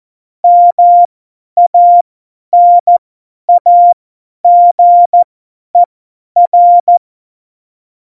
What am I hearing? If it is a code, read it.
Morse code, heard as a keyed tone, about 14 wpm, spaced as for 10 wpm, 700 Hz. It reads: MANAGER